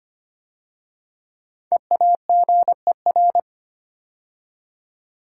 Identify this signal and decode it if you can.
Morse code, heard as a keyed tone, about 25 wpm, 700 Hz. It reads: EAGER
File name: 5 Letter Words - First 3 Letters Given - 25wpm